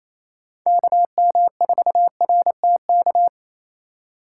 Morse code, heard as a keyed tone, about 28 wpm, 700 Hz. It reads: KM4RTK
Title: USA Call Signs - 6 Characters - 8x Character Spacing - 28wpm